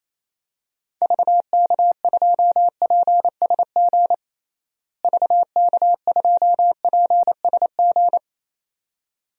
Morse code, heard as a keyed tone, about 28 wpm, 700 Hz. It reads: VK2PSG VK2PSG
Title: DXCC-100 Call Signs - 6 Characters - Repeated 2x - 28wpm